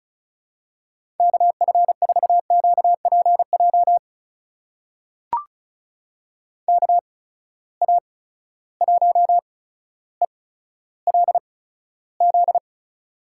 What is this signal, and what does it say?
Morse code, heard as a keyed tone, about 36 wpm, spaced as for 15 wpm, 700 Hz. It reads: KF4QPJ KA1ELZ
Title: USA Call Signs - 6 Characters - 8x Character Spacing - 35wpm